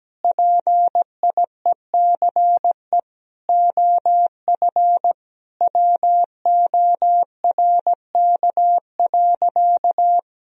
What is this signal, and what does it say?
Morse code, heard as a keyed tone, about 17 wpm, 700 Hz. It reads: PIECE OF WORK.
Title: Sentences from Top 1000 Words - Review Part 2 - 17wpm